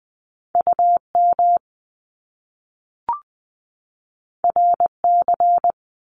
Morse code, heard as a keyed tone, about 20 wpm, 700 Hz. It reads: UM RC
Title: Top Two Letter Combinations in Words 20wpm